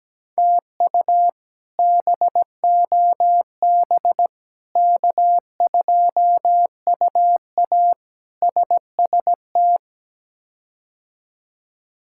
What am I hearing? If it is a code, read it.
Morse code, heard as a keyed tone, about 17 wpm, 700 Hz. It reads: TU BOB K2UA SST
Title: K1USN SST QSO Practice - 17wpm